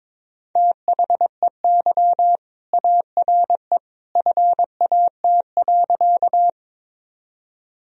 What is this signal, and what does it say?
Morse code, heard as a keyed tone, about 22 wpm, 700 Hz. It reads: THEY ARE FAT.
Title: Sentences from Top 1000 Words - Review Part 4 - 22wpm